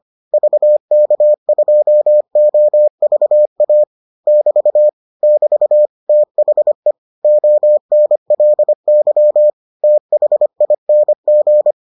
Morse code, heard as a keyed tone, about 25 wpm, 600 Hz. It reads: VK2OVA <BT> <BT> THE ONLY THING